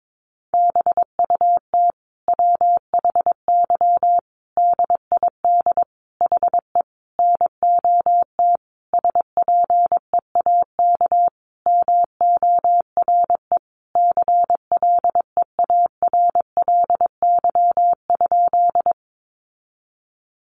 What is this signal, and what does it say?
Morse code, heard as a keyed tone, about 22 wpm, 700 Hz. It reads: BUT WHY DID HE NOT SPEAK MORE CLEARLY?
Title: Sentences from Top 1300 Words 22wpm